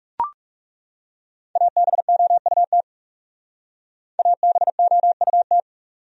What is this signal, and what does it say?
Morse code, heard as a keyed tone, about 37 wpm, 700 Hz. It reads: ABOUT ABOUT
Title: Speed-Racing Top 300 Words 30wpm